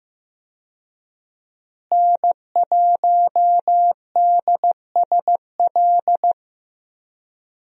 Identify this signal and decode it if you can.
Morse code, heard as a keyed tone, about 15 wpm, 700 Hz. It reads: N1DSL
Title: USA Call Signs - 5 Characters - Repeated 2x - 15wpm